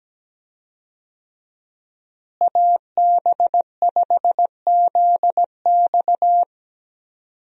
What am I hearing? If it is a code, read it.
Morse code, heard as a keyed tone, about 17 wpm, 700 Hz. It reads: AB5ZX
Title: USA Call Signs - 5 Characters - Repeated 3x - 17wpm